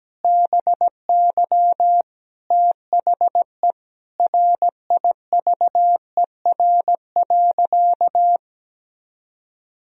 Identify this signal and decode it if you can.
Morse code, heard as a keyed tone, about 17 wpm, 700 Hz. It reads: BY THE RIVER.
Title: Sentences from Top 1000 Words - Review Part 1 - 17wpm